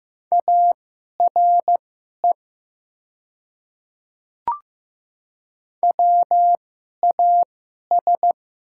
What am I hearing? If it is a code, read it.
Morse code, heard as a keyed tone, about 15 wpm, spaced as for 11 wpm, 700 Hz. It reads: ARE WAS
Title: 2x Character Spacing - Top 100 Words - 15wpm